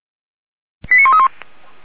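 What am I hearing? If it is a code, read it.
Five-tone selective call (ZVEI), heard as quick runs of five tones, tones of about 70 ms. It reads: 89131
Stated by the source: ZVEI five-tone selective call 89227, received on SDR, fire brigade Orlamünde (feuerwehr-orlamuende).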